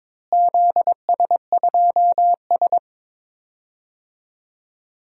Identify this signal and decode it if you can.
Morse code, heard as a keyed tone, about 22 wpm, 700 Hz. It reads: ZS2S
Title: DXCC-100 Call Signs - 4 Characters - Repeated 2x - 22wpm